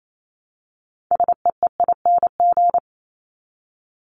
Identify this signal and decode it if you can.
Morse code, heard as a keyed tone, about 28 wpm, 700 Hz. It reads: SEEING
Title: ING Suffix - Top 100 Words with Base Words 28wpm